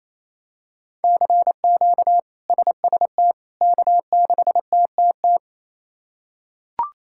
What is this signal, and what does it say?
Morse code, heard as a keyed tone, about 28 wpm, 700 Hz. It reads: CQ SST K6TTT
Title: K1USN SST QSO Practice - 28wpm